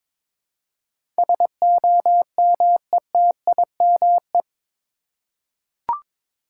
Morse code, heard as a keyed tone, about 22 wpm, 700 Hz. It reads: SOMETIME
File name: practice / 22wpm / Eight Letter Words - 22wpm